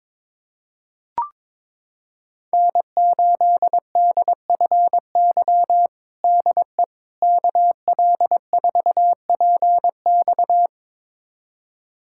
Morse code, heard as a keyed tone, about 22 wpm, 700 Hz. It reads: N8DFY DE KL4PX
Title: USA Callsign DE Callsign - 5 Characters - 22wpm